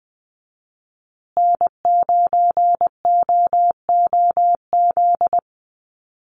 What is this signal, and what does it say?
Morse code, heard as a keyed tone, about 20 wpm, 700 Hz. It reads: N9OOZ
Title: USA Call Signs Repeated 3x 20wpm